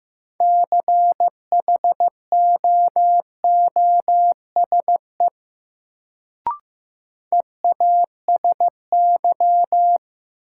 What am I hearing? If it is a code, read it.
Morse code, heard as a keyed tone, about 15 wpm, 700 Hz. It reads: CHOOSE EASY
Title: Sentences from Top 700 Words - New Word Drill - 15wpm